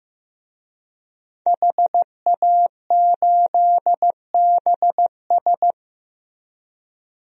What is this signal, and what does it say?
Morse code, heard as a keyed tone, about 15 wpm, 700 Hz. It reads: HA8BS
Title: DXCC-100 Call Signs - 5 Characters - Repeated 2x - 15wpm